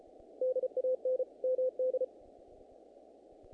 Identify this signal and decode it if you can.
Morse code, heard as a keyed tone, about 33 wpm, 510 Hz. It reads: DAN MD
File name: contests / naqp-cw-k5tr-op-k5pi- -2000